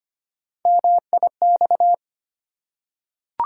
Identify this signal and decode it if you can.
Morse code, heard as a keyed tone, about 25 wpm, 700 Hz. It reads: MIX
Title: Three Letter Words - 25wpm